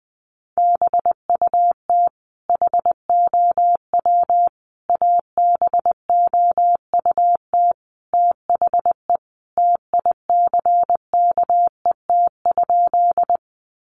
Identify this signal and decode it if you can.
Morse code, heard as a keyed tone, about 20 wpm, 700 Hz. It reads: BUT HOW ABOUT THE TICKET?
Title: Sentences from Top 1400 Words 20wpm